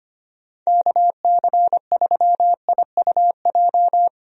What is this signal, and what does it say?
Morse code, heard as a keyed tone, about 25 wpm, 700 Hz. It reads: KC3IUJ
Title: USA Call Signs - 6 Characters - 25wpm